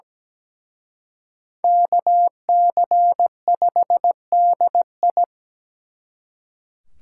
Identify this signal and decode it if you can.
Morse code, heard as a keyed tone, about 20 wpm, 700 Hz. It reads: KC5DI
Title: Speed Racing - USA 2x2 Call Signs 17wpm